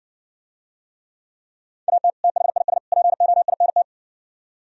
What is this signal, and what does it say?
Morse code, heard as a keyed tone, about 68 wpm, 700 Hz. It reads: AT THIS POINT